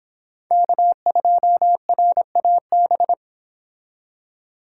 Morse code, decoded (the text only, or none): K2RAB